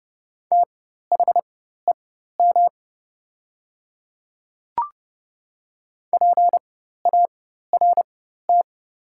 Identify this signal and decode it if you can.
Morse code, heard as a keyed tone, about 30 wpm, spaced as for 14 wpm, 700 Hz. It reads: THEM PART
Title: 4x Character Spacing - Top 200 Words - 30wpm